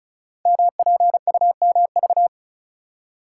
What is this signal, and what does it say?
Morse code, heard as a keyed tone, about 36 wpm, 700 Hz. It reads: MPUMV
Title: Code Groups - 5 Letters - 35wpm